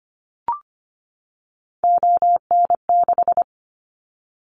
Morse code, heard as a keyed tone, about 25 wpm, 700 Hz. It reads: ON6
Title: Callsign Prefixes for Top 25 DXCC - 25wpm